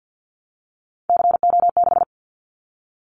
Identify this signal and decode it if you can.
Morse code, heard as a keyed tone, about 48 wpm, 700 Hz. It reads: CO6
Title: Callsign Prefixes for Top 100 DXCC - 50wpm